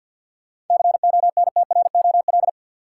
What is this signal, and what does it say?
Morse code, heard as a keyed tone, about 53 wpm, 700 Hz. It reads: CONTROL
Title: Sentences from Top 1200 Words - New Word Drill Alt - 50wpm